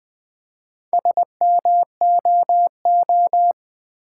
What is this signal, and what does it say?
Morse code, heard as a keyed tone, about 20 wpm, 700 Hz. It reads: SMOO
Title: Build A Word - 2 Letters At A Time - 6-Letter Words - 20wpm